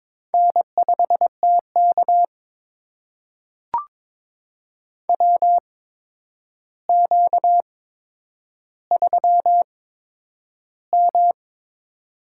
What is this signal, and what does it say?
Morse code, heard as a keyed tone, about 22 wpm, spaced as for 9 wpm, 700 Hz. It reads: N5TK WQ3M
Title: USA Call Signs - 4 Characters - 8x Character Spacing - 22wpm